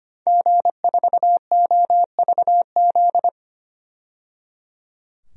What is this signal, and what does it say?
Morse code, heard as a keyed tone, about 25 wpm, 700 Hz. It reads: G4OVZ
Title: DXCC-100 Call Signs - 5 Characters - 25wpm